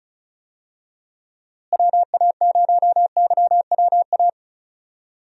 Morse code, heard as a keyed tone, about 36 wpm, 700 Hz. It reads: WA0YWA
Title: USA Call Signs - 6 Characters - Repeated 2x - 35wpm